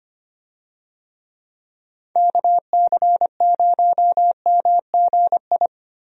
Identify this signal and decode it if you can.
Morse code, heard as a keyed tone, about 25 wpm, 700 Hz. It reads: KC0MGI
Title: USA Call Signs - 6 Characters - Repeated 3x - 25wpm